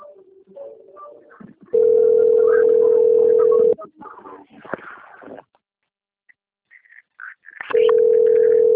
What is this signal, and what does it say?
Ringback tone (North America), 440 + 480 Hz.
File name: out- - - - - .42324